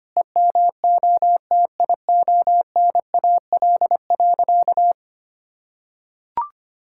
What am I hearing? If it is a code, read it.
Morse code, heard as a keyed tone, about 25 wpm, 700 Hz. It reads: EMOTIONAL.
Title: Sentences from Top 1000 Words - Review Part 5 - 25wpm